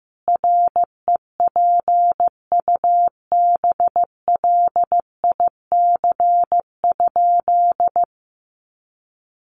Morse code, heard as a keyed tone, about 15 wpm, 700 Hz. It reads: REPUBLIC?
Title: Sentences from Top 1500 Words 15wpm